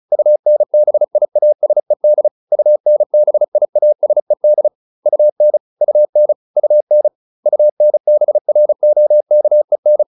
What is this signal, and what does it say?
Morse code, heard as a keyed tone, about 34 wpm, 600 Hz. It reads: UNBIASED UNBIASED UN UN UN UNBROKEN
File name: UN8 35